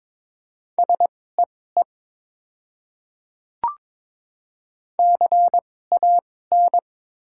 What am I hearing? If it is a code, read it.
Morse code, heard as a keyed tone, about 22 wpm, spaced as for 16 wpm, 700 Hz. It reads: SEE CAN
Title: 2x Character Spacing - Top 100 Words - 22wpm